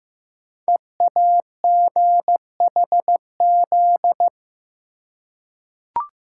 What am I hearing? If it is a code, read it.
Morse code, heard as a keyed tone, about 15 wpm, 700 Hz. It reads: EAGHZ
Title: Code Groups - 5 Letters - 15wpm